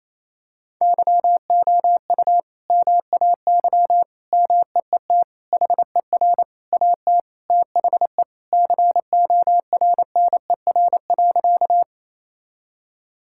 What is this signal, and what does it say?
Morse code, heard as a keyed tone, about 28 wpm, 700 Hz. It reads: YOU MAY MEET HER AT THE CORNER.
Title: Sentences from Top 1600 Words 28wpm